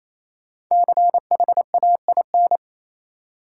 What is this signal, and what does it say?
Morse code, heard as a keyed tone, about 28 wpm, 700 Hz. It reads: CHAIN
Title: Sentences from Top 1500 Words - New Word Drill - 28wpm